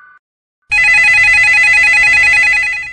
There is no code.